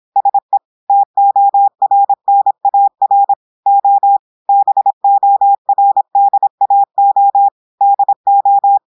Morse code, heard as a keyed tone, about 26 wpm, 800 Hz. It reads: SE TORNAR O BORDAO DO
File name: portugais texte en clair 2 vitesse 26 mots minutes